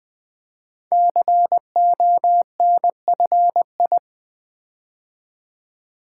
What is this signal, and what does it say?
Morse code, heard as a keyed tone, about 20 wpm, 700 Hz. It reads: CONFI
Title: Build A Word - 5 Letters At A Time - 12-Letter Words - 20wpm